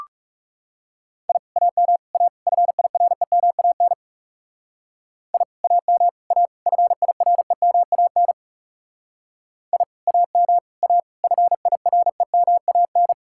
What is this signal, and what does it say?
Morse code, heard as a keyed tone, about 39 wpm, 700 Hz. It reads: I AM A FIREMAN I AM A FIREMAN I AM A FIREMAN